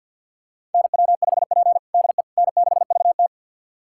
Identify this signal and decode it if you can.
Morse code, heard as a keyed tone, about 51 wpm, 700 Hz. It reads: NW5P DE N6VT
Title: USA Callsign DE Callsign - 4 Characters - 50wpm